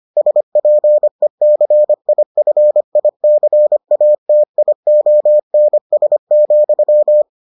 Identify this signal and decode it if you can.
Morse code, heard as a keyed tone, about 25 wpm, 600 Hz. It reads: SPECIFICATIONS,